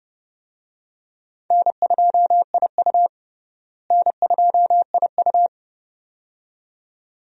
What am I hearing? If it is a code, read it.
Morse code, heard as a keyed tone, about 30 wpm, 700 Hz. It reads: N2IU N2IU
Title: USA Call Signs - 4 Characters - Repeated 2x - 30wpm